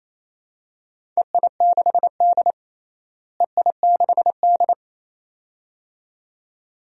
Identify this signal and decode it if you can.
Morse code, heard as a keyed tone, about 28 wpm, 700 Hz. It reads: EI6D EI6D
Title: DXCC-100 Call Signs - 4 Characters - Repeated 2x - 28wpm